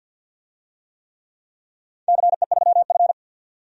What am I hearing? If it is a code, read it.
Morse code, heard as a keyed tone, about 51 wpm, 700 Hz. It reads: CE3F
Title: DXCC-100 Call Signs - 4 Characters - 50wpm